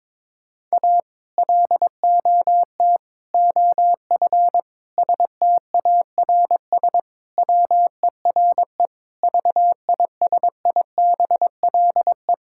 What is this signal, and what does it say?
Morse code, heard as a keyed tone, about 22 wpm, 700 Hz. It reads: A LOT OF STARS WERE VISIBLE
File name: practice / 22wpm / Sentences with 2-Word Phrases - 22wpm